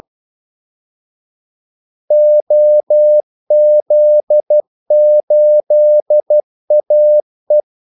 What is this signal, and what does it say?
Morse code, heard as a keyed tone, about 12 wpm, 600 Hz. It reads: OZ8AE
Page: 2 ekstra ordmellemrum mellem kaldesignalerne